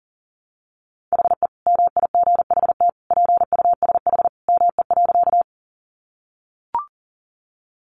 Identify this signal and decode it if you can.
Morse code, heard as a keyed tone, about 40 wpm, 700 Hz. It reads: HE MIGHT PUSH ME.
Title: Sentences from Top 1300 Words 40wpm